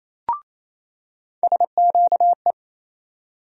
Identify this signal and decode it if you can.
Morse code, heard as a keyed tone, about 27 wpm, 700 Hz. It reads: SQE
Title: USA Call Sign Postfixes - 3 Letters 28wpm